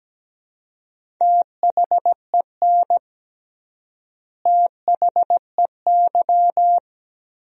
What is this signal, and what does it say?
Morse code, heard as a keyed tone, about 17 wpm, 700 Hz. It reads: THEN THEY
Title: 3x Word-Spacing - Sentences from Top 200 Words - 17wpm